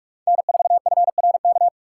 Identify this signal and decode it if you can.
Morse code, heard as a keyed tone, about 44 wpm, 700 Hz. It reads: N4FRK